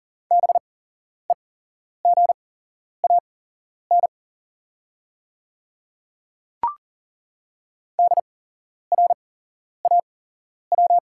Morse code, heard as a keyed tone, about 40 wpm, spaced as for 11 wpm, 690 Hz. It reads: BEGAN DRAW
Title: 8x Character Spacing - Top 300 Words - 40wpm